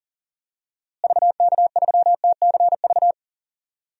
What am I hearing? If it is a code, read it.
Morse code, heard as a keyed tone, about 37 wpm, 700 Hz. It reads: VK3TCV
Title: DXCC-100 Call Signs - 6 Characters - Repeated 2x - 40wpm